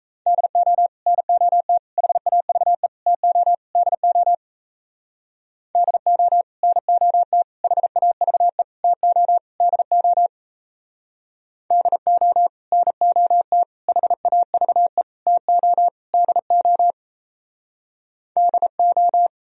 Morse code, heard as a keyed tone, about 37 wpm, 700 Hz. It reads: DO NOT HAVE TO DO DO NOT HAVE TO DO DO NOT HAVE TO DO DO